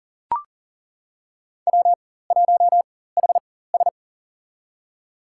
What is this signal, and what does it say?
Morse code, heard as a keyed tone, about 40 wpm, spaced as for 19 wpm, 700 Hz. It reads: W1HS
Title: USA Call Signs - 4 Characters - 4x Character Spacing - 40wpm